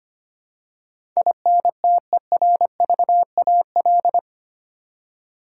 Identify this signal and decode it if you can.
Morse code, heard as a keyed tone, about 25 wpm, 690 Hz. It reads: INTERVAL
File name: 8 Letter Words - First 3 Letters Given - 25wpm